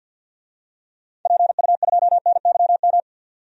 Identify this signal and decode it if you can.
Morse code, heard as a keyed tone, about 46 wpm, 700 Hz. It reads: PU2NYM
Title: DXCC-100 Call Signs - 6 Characters - 50wpm